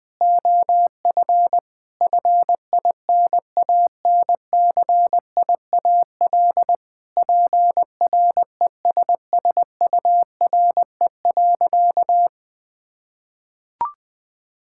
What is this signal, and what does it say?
Morse code, heard as a keyed tone, about 21 wpm, 700 Hz. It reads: OF FINANCIAL PRESSURE.
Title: Sentences from Top 1000 Words - Review Part 6 - 20wpm